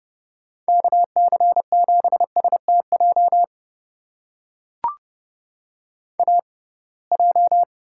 Morse code, heard as a keyed tone, about 30 wpm, spaced as for 23 wpm, 700 Hz. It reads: KC7STJ AJ
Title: USA Call Signs - 6x Character Spacing - 30wpm